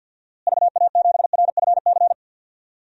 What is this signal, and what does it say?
Morse code, heard as a keyed tone, about 52 wpm, 700 Hz. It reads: VA7RFC
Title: DXCC-100 Call Signs - 6 Characters - Repeated 3x - 50wpm